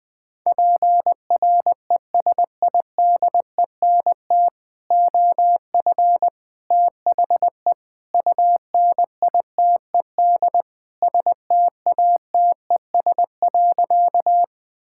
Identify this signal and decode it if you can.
Morse code, heard as a keyed tone, about 21 wpm, 700 Hz. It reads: PRESIDENT OF THE UNITED STATES.
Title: Sentences with 4-Word Phrases - 20wpm